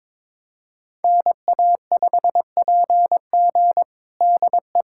Morse code, heard as a keyed tone, about 21 wpm, 700 Hz. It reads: NA5PG DE
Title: USA Callsign DE Callsign - 5 Characters - 22wpm